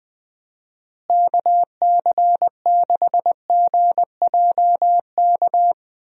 Morse code, heard as a keyed tone, about 19 wpm, 700 Hz. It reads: KC6GJK
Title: USA Call Signs - 6 Characters - Repeated 2x - 20wpm